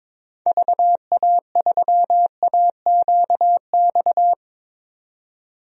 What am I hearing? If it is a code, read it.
Morse code, heard as a keyed tone, about 22 wpm, 700 Hz. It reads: VA3AQX
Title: DXCC-100 Call Signs - 6 Characters - 22wpm